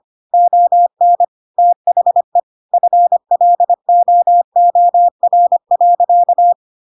Morse code, heard as a keyed tone, about 25 wpm, 700 Hz. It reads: ON THE FLOOR.
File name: Pinocchio / Pinocchio ch7 25 25 ik1hge